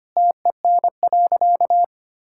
Morse code, heard as a keyed tone, about 25 wpm, 700 Hz. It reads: TEN.
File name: practice / 25wpm / Sentences with 4-Word Phrases - 25wpm